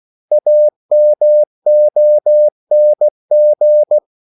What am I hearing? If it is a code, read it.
Morse code, heard as a keyed tone, about 16 wpm, 600 Hz. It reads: AMONG
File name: CW / Practice / 500MostCommonEnglishWords16